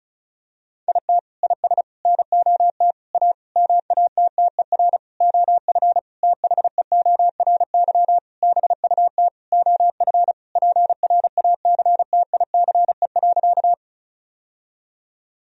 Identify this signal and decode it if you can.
Morse code, heard as a keyed tone, about 36 wpm, 700 Hz. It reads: IT IS NOT A MATTER OF THEORY BUT OF PRACTICE.
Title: Sentences from Top 1000 Words - Review Part 4 - 35wpm